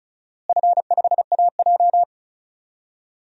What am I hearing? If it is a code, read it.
Morse code, heard as a keyed tone, about 34 wpm, 700 Hz. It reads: F5AJ
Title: DXCC-100 Call Signs - 4 Characters - Repeated 3x - 35wpm